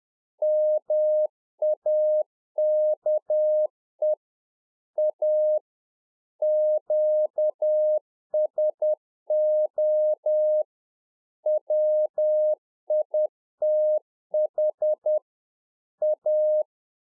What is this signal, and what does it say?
Morse code, heard as a keyed tone, about 10 wpm, 610 Hz. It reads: MAKE A QSO WITH A
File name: UNFOGETTABLE QSO 10WPM